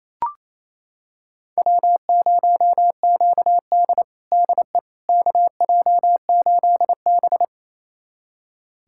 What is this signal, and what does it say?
Morse code, heard as a keyed tone, about 28 wpm, 690 Hz. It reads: W0QD DE KJ8B
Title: USA Callsign DE Callsign - 4 Characters - 28wpm